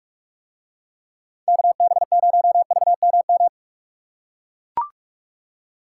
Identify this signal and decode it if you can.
Morse code, heard as a keyed tone, about 40 wpm, 700 Hz. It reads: KB0VMM